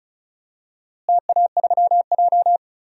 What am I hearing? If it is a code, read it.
Morse code, heard as a keyed tone, about 34 wpm, 700 Hz. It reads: TA3J
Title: DXCC-100 Call Signs - 4 Characters - Repeated 3x - 35wpm